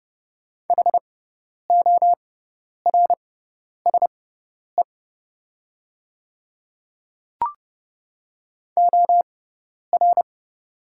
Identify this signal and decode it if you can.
Morse code, heard as a keyed tone, about 30 wpm, spaced as for 10 wpm, 700 Hz. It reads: HORSE OR
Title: 6x Character Spacing - Top 300 Words - 30wpm